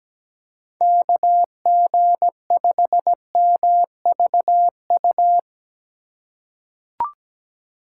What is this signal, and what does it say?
Morse code, heard as a keyed tone, about 17 wpm, 700 Hz. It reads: KG5MVU
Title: USA Call Signs - 6 Characters - 3x Character Spacing - 17wpm